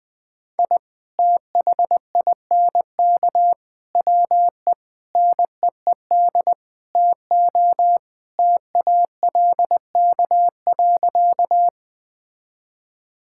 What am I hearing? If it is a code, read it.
Morse code, heard as a keyed tone, about 21 wpm, 700 Hz. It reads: I THINK WE NEED TO TALK.
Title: Sentences with 5-Word Phrases - Spoken Beginning - 20wpm